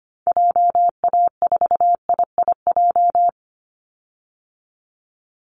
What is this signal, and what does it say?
Morse code, heard as a keyed tone, about 25 wpm, 700 Hz. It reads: JA4IIJ
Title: Call Signs Repeated 2x - 25wpm